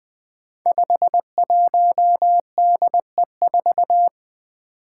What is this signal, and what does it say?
Morse code, heard as a keyed tone, about 20 wpm, 700 Hz. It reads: 51DE4
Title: Five Challenging Letter-Number Combinations 20wpm